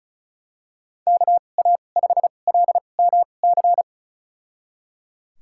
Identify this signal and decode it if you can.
Morse code, heard as a keyed tone, about 36 wpm, spaced as for 25 wpm, 700 Hz. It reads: KA5LMC
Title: USA Call Signs - 6 Characters - 2x Character Spacing - 35wpm